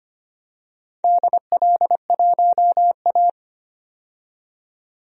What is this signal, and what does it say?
Morse code, heard as a keyed tone, about 25 wpm, 700 Hz. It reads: DL1A
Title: DXCC-100 Call Signs - 4 Characters - 25wpm